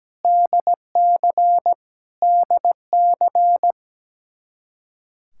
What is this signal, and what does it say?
Morse code, heard as a keyed tone, about 17 wpm, 700 Hz. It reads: DC DC
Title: Practice QSO - Exchange 1 - Beginning - 17wpm